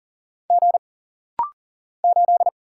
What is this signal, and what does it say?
Morse code, heard as a keyed tone, about 40 wpm, 700 Hz. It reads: G 8
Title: Single Letter-Number - Rapid-Fire 40wpm